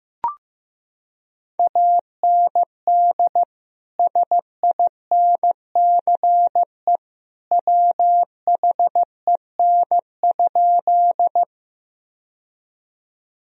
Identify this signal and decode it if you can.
Morse code, heard as a keyed tone, about 15 wpm, 700 Hz. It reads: AND SINCE WHEN?
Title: Sentences from Top 1000 Words - Review Part 1 - 15wpm